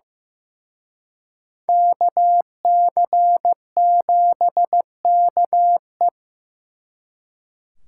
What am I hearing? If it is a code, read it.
Morse code, heard as a keyed tone, about 18 wpm, 700 Hz. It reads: KC7KE
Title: Speed Racing - USA 2x2 Call Signs 15wpm